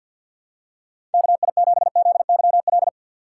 Morse code, heard as a keyed tone, about 49 wpm, 700 Hz. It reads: KI7ZYL
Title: USA Call Signs - 6 Characters - 6x Character Spacing - 50wpm